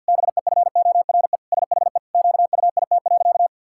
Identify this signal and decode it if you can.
Morse code, heard as a keyed tone, about 46 wpm, 700 Hz. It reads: BEFORE SHE QUIT.
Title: Pangram Challenge v2 - 50wpm